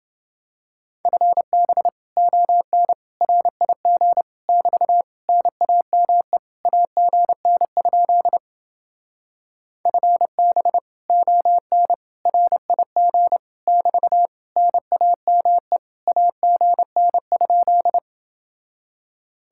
Morse code, heard as a keyed tone, about 28 wpm, 700 Hz. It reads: FB ON RIG <BT> NAME AGN? FB ON RIG <BT> NAME AGN?